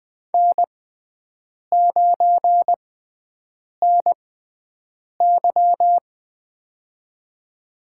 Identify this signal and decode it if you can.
Morse code, heard as a keyed tone, about 20 wpm, spaced as for 7 wpm, 700 Hz. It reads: N9NY
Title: USA Call Signs - 6x Character Spacing - 20wpm